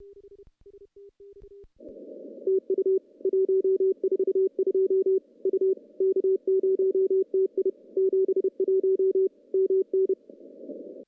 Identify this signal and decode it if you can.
Morse code, heard as a keyed tone, about 31 wpm, 380 Hz. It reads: TU 142 U K0TI 71 MN
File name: contests / CW-SS-k5tr- -0100